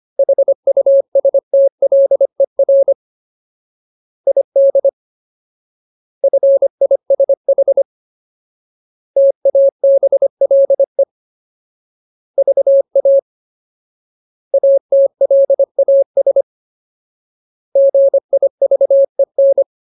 Morse code, heard as a keyed tone, about 25 wpm, 550 Hz. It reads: HUSTLER ID FISH TABLE VA ATLAS GIVEN
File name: Ultimate Word List / 25WPM Ultimate